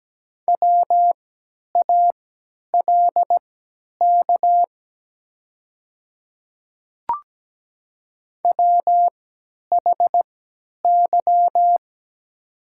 Morse code, heard as a keyed tone, about 17 wpm, spaced as for 10 wpm, 700 Hz. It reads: WALK WHY
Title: 3x Character Spacing - Top 300 Words - 17wpm